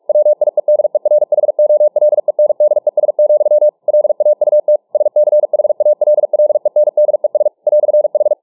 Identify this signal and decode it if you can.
Morse code, heard as a keyed tone, about 45 wpm, 600 Hz. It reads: WIEDERHOLENDES, LAUT SCHALLENDES "H